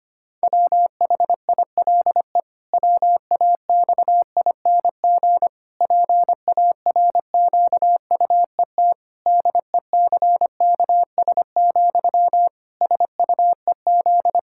Morse code, heard as a keyed tone, about 25 wpm, 700 Hz. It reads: WHILE WAXING PARQUET DECKS, SUEZ
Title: Pangram Challenge v2 - 25wpm